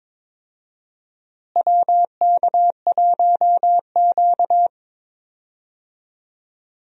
Morse code, heard as a keyed tone, about 22 wpm, 700 Hz. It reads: WK1Q
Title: USA Call Signs - 4 Characters - Repeated 2x - 22wpm